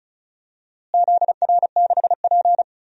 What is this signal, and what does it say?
Morse code, heard as a keyed tone, about 36 wpm, 700 Hz. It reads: ZR6P
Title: DXCC-100 Call Signs - 4 Characters - Repeated 3x - 35wpm